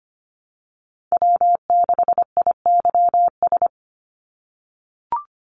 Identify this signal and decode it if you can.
Morse code, heard as a keyed tone, about 25 wpm, 700 Hz. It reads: W6IYS
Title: Call Signs - 25wpm